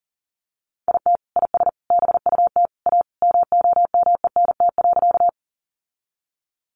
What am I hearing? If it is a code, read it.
Morse code, heard as a keyed tone, about 40 wpm, 700 Hz. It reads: IT IS BUT A MOMENT.
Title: Sentences from Top 1000 Words 40wpm